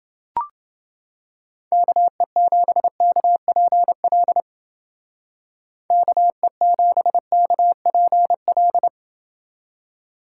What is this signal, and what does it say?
Morse code, heard as a keyed tone, about 29 wpm, 700 Hz. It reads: KE7KPL KE7KPL